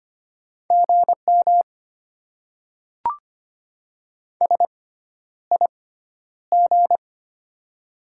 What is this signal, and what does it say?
Morse code, heard as a keyed tone, about 25 wpm, spaced as for 11 wpm, 700 Hz. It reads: GM SIG
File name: Top 90 QSO Elements - 6x Character Spacing - 25wpm